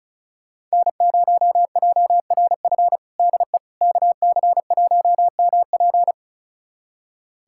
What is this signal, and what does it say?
Morse code, heard as a keyed tone, about 36 wpm, 700 Hz. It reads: N0JRF DE KC1MP
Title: USA Callsign DE Callsign - 5 Characters - 35wpm